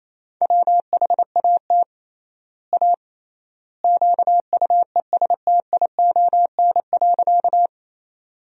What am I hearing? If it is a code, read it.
Morse code, heard as a keyed tone, about 28 wpm, 700 Hz. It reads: WHAT A QUESTION.
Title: 3x Word-Spacing - Sentences from Top 400 Words - 28wpm